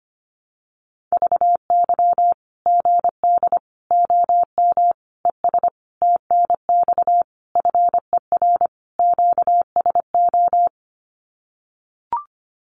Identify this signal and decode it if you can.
Morse code, heard as a keyed tone, about 25 wpm, 700 Hz. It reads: VY GD OM ES TNX FER QSO
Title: Common QSO Phrases 25wpm